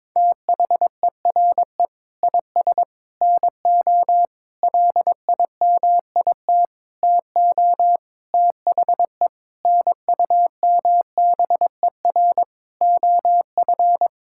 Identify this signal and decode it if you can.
Morse code, heard as a keyed tone, about 22 wpm, 700 Hz. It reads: THERE IS NO LIMIT TO THE NUMBER OF
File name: practice / 22wpm / Sentences with 8-Word Phrases - 22wpm